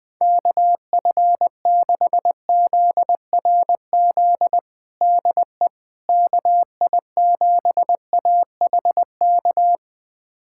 Morse code, heard as a keyed tone, about 20 wpm, 700 Hz. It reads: KF6ZRZ DE KI7AHK
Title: USA Callsign DE Callsign 20wpm